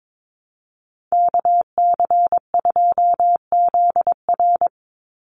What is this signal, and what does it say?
Morse code, heard as a keyed tone, about 22 wpm, 700 Hz. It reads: KC2ZR
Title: USA Call Signs 22wpm